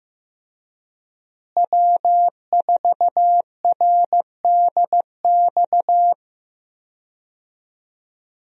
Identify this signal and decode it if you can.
Morse code, heard as a keyed tone, about 15 wpm, 700 Hz. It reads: W4RDX
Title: USA Call Signs - 5 Characters - Repeated 2x - 15wpm